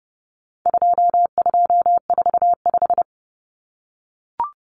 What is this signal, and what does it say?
Morse code, heard as a keyed tone, about 30 wpm, 700 Hz. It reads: 2245
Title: Numbers - 4-Digit 30wpm